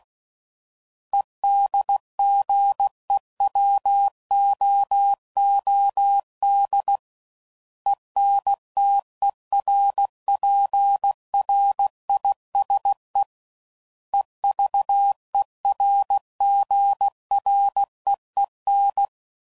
Morse code, heard as a keyed tone, about 16 wpm, 800 Hz. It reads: EDGEWOOD ENTERPRISE EVERGREEN
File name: CW Code Groups @15WPM